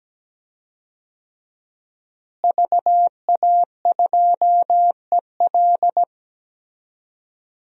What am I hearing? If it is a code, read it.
Morse code, heard as a keyed tone, about 17 wpm, 700 Hz. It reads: VA2EL
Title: DXCC-100 Call Signs - 5 Characters - Repeated 2x - 17wpm